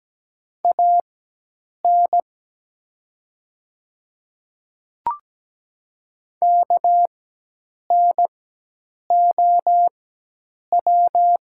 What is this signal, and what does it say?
Morse code, heard as a keyed tone, about 17 wpm, spaced as for 8 wpm, 700 Hz. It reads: AN KNOW
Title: 4x Character Spacing - Top 100 Words - 17wpm